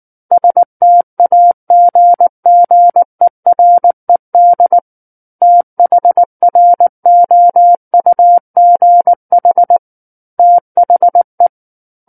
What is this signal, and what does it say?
Morse code, heard as a keyed tone, about 19 wpm, 700 Hz. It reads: STAGGERED THROUGH THE
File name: Chapter 10 - 19 WPM